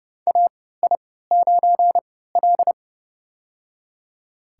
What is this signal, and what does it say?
Morse code, heard as a keyed tone, about 30 wpm, spaced as for 17 wpm, 700 Hz. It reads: AI9L
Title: USA Call Signs - 4 Characters - 3x Character Spacing - 30wpm